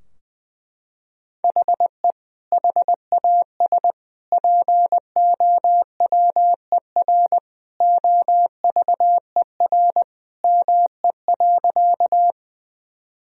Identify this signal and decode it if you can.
Morse code, heard as a keyed tone, about 20 wpm, 700 Hz. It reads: HE HAS POWER OVER ME.
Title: Sentences from Top 1000 Words - Review Part 2 - 20wpm